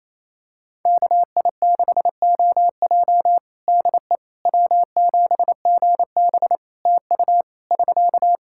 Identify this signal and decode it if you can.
Morse code, heard as a keyed tone, about 27 wpm, 700 Hz. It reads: KI6OJ DE W7GB TU <SK>
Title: Practice QSO - Exchange 4 - Ending - 28wpm